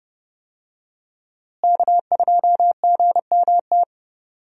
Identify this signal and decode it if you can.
Morse code, heard as a keyed tone, about 31 wpm, 700 Hz. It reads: K2GMT
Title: USA Call Signs - 5 Characters - Repeated 3x - 30wpm